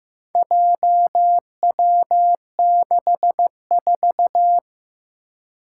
Morse code, heard as a keyed tone, about 15 wpm, 700 Hz. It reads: JW64
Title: Four Challenging Letter-Number Combinations 15wpm